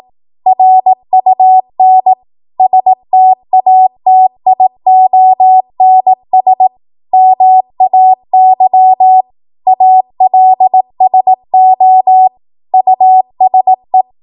Morse code, heard as a keyed tone, about 18 wpm, 750 Hz. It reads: RUN STATIONS MAY ALSO USE
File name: Archive / 18 WPM / 18WPM